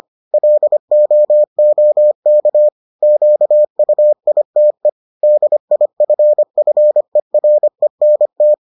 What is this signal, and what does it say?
Morse code, heard as a keyed tone, about 25 wpm, 600 Hz. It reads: LOOK QUITE DIFFERENT